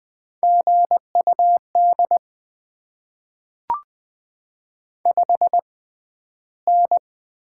Morse code, heard as a keyed tone, about 20 wpm, spaced as for 12 wpm, 690 Hz. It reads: GUD 5N
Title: Top 20 QSO Elements - 6x Character Spacing - 20wpm